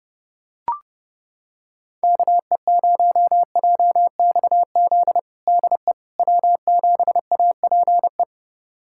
Morse code, heard as a keyed tone, about 30 wpm, 700 Hz. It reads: KE0JXZ DE W7APE
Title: USA Callsign DE Callsign 30wpm